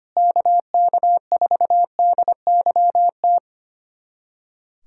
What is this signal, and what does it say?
Morse code, heard as a keyed tone, about 25 wpm, 700 Hz. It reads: KK4DYT
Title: USA Callsign DE Callsign 25wpm